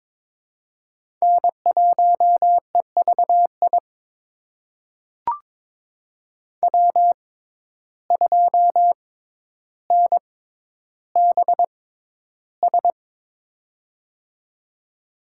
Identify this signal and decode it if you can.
Morse code, heard as a keyed tone, about 22 wpm, spaced as for 11 wpm, 700 Hz. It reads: N1EVI W2NBS
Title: USA Call Signs - 5 Characters - 6x Character Spacing - 22wpm